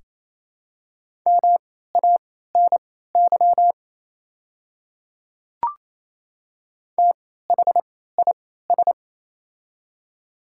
Morse code, heard as a keyed tone, about 28 wpm, spaced as for 16 wpm, 700 Hz. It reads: MANY THIS
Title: 3x Character Spacing - Top 100 Words - 28wpm